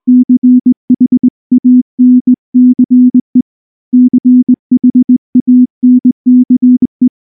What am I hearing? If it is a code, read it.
Morse code, heard as a keyed tone, about 19 wpm, 250 Hz. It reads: CHANCE CHANCE
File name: morse / morse-training-20WPM-4